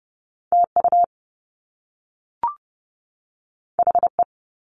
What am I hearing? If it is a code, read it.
Morse code, heard as a keyed tone, about 30 wpm, 700 Hz. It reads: TU HE
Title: Top Two Letter Combinations in Words 30wpm